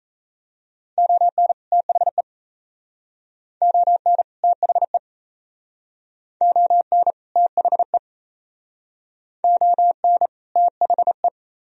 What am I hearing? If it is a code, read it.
Morse code, heard as a keyed tone, about 37 wpm, 700 Hz. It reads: ON THE ON THE ON THE ON THE